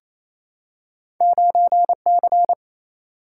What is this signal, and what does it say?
Morse code, heard as a keyed tone, about 28 wpm, 700 Hz. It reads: 9 C
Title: Single-Letter Single-Number 28wpm